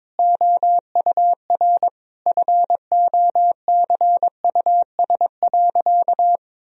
Morse code, heard as a keyed tone, about 22 wpm, 700 Hz. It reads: OUR FOCUS.
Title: Sentences from Top 1000 Words - Review Part 4 - 22wpm